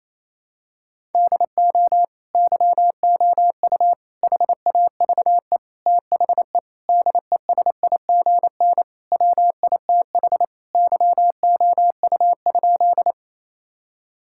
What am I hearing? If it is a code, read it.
Morse code, heard as a keyed tone, about 28 wpm, 700 Hz. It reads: DO YOU HAVE THE DESIGN WITH YOU?
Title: Sentences from Top 1000 Words - Review Part 5 - 28wpm